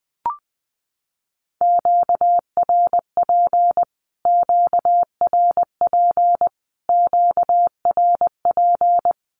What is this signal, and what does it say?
Morse code, heard as a keyed tone, about 20 wpm, 700 Hz. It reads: QRP QRP QRP
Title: Q-Codes Repeated 3x 20wpm